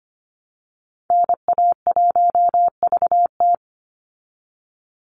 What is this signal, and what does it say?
Morse code, heard as a keyed tone, about 25 wpm, 700 Hz. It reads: NA1VT
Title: Call Signs Repeated 3x - 25wpm